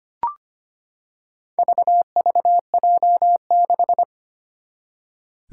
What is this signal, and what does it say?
Morse code, heard as a keyed tone, about 25 wpm, 700 Hz. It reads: VVJ6
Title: Four Challenging Letter-Number Combinations 25wpm